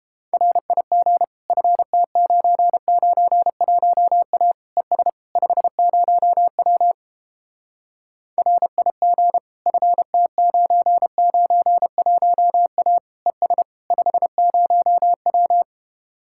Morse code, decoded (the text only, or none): RIG FT991A ES 50W RIG FT991A ES 50W